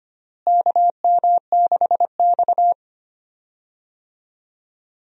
Morse code, heard as a keyed tone, about 25 wpm, 700 Hz. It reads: KM6X
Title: USA Call Signs - 4 Characters - Repeated 3x - 25wpm